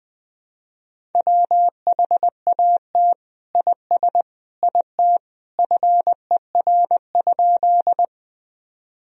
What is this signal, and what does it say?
Morse code, heard as a keyed tone, about 20 wpm, 700 Hz. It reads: WHAT IS IT FER?
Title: Sentences from Top 100 Words and Abbreviations 20wpm